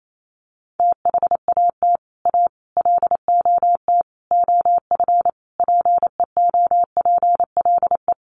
Morse code, heard as a keyed tone, about 28 wpm, 700 Hz. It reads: THAT A LOT OF PEOPLE
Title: Sets of 5 Words 28wpm